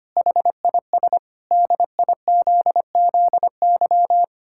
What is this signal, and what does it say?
Morse code, heard as a keyed tone, about 25 wpm, 700 Hz. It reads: HIS DIZZY
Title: Pangram Challenge v2 - 25wpm